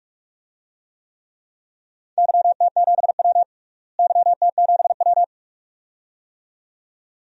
Morse code, heard as a keyed tone, about 46 wpm, 700 Hz. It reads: YT7W YT7W
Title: DXCC-100 Call Signs - 4 Characters - Repeated 2x - 45wpm